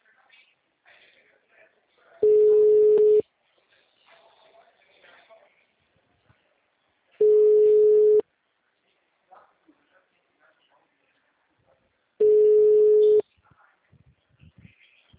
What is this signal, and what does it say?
Ringback tone (Europe, 425 Hz), with speech.
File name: out- -2012- - - .66322